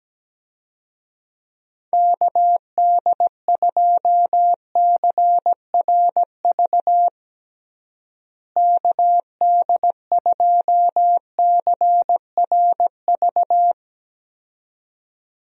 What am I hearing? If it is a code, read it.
Morse code, heard as a keyed tone, about 17 wpm, 700 Hz. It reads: KD2CRV KD2CRV
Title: USA Call Signs - 6 Characters - Repeated 2x - 17wpm